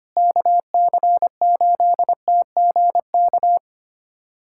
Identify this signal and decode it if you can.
Morse code, heard as a keyed tone, about 25 wpm, 700 Hz. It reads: KC8TGK
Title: USA Callsign DE Callsign - 6 Characters - 25wpm